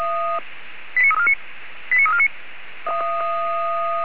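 Five-tone selective call (ZVEI), heard as quick runs of five tones, tones of about 80 ms. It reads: 89349 89349
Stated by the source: ZVEI five-tone selective call received via SDR.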